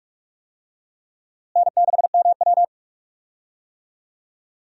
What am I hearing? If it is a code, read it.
Morse code, heard as a keyed tone, about 46 wpm, 700 Hz. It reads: N6MW
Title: USA Call Signs - 4 Characters - Repeated 3x - 45wpm